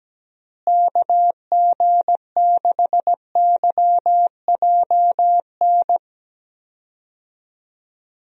Morse code, heard as a keyed tone, about 17 wpm, 700 Hz. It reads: KG6YJN
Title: USA Call Signs - 6 Characters - Repeated 2x - 17wpm